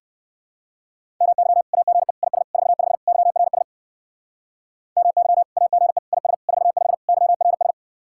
Morse code, heard as a keyed tone, about 56 wpm, 700 Hz. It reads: MY AGE IS 45 YRS MY AGE IS 45 YRS
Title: Speed-Racing Common QSO Phrases Encore - 45wpm